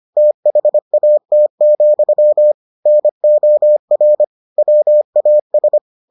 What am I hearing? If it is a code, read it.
Morse code, heard as a keyed tone, about 25 wpm, 600 Hz. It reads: THAT, NOR WAS